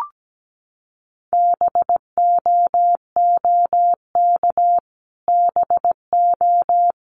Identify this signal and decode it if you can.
Morse code, heard as a keyed tone, about 17 wpm, 700 Hz. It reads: BOOK BO
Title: Top 500 Words Repeated 17wpm